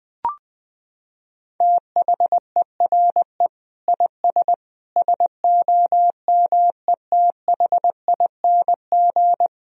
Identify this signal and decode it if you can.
Morse code, heard as a keyed tone, about 20 wpm, 700 Hz. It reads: THERE IS SOMETHING
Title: Sentences from Top 1000 Words - Review Part 3 - 20wpm